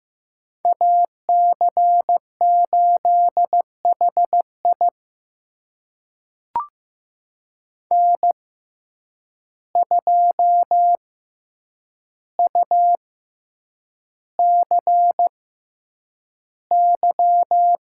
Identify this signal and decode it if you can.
Morse code, heard as a keyed tone, about 15 wpm, spaced as for 8 wpm, 700 Hz. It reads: AC8HI N2UCY
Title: USA Call Signs - 5 Characters - 6x Character Spacing - 15wpm